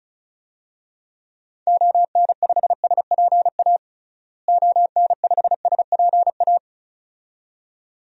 Morse code, heard as a keyed tone, about 34 wpm, 700 Hz. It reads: ON5SPA ON5SPA
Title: DXCC-100 Call Signs - 6 Characters - Repeated 2x - 35wpm